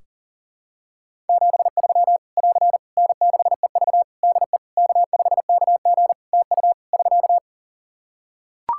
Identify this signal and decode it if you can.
Morse code, heard as a keyed tone, about 40 wpm, 700 Hz. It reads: 73 <AR> N6EV DE K5KG TU <SK>
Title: K1USN SST QSO Practice 40wpm